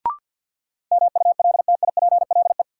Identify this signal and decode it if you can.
Morse code, heard as a keyed tone, about 52 wpm, 700 Hz. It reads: MULTIPLE
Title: Sets of 5 Words - New Word Drill - 50wpm